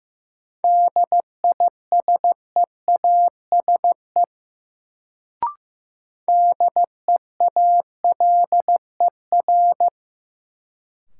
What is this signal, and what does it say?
Morse code, heard as a keyed tone, about 15 wpm, 700 Hz. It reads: DISEASE DEALER
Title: Sentences from Top 1100 Words - New Word Drill Alt - 15wpm